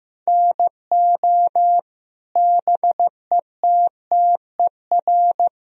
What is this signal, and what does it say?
Morse code, heard as a keyed tone, about 15 wpm, 700 Hz. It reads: NO BETTER
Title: Sentences from Top 1000 Words - Review Part 4 - 15wpm